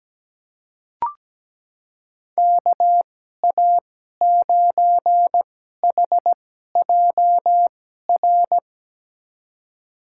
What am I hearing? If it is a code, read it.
Morse code, heard as a keyed tone, about 17 wpm, spaced as for 12 wpm, 700 Hz. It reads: KA9HJR
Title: USA Call Signs - 6 Characters - 2x Character Spacing - 17wpm